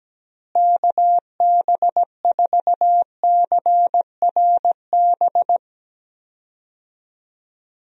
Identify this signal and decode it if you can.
Morse code, heard as a keyed tone, about 17 wpm, 700 Hz. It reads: KB4CRB
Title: USA Call Signs - 6 Characters - Repeated 2x - 17wpm